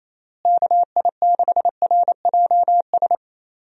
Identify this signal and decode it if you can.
Morse code, heard as a keyed tone, about 27 wpm, 700 Hz. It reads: KI6RJS
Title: USA Call Signs - 6 Characters - Repeated 2x - 28wpm